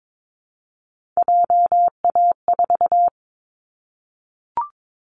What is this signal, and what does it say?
Morse code, heard as a keyed tone, about 22 wpm, 700 Hz. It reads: JA4
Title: Callsign Prefixes for Top 25 DXCC - Repeated 3x - 22wpm